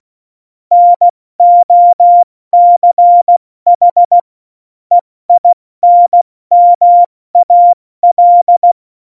Morse code, heard as a keyed tone, about 16 wpm, 700 Hz. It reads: NOCH EINMAL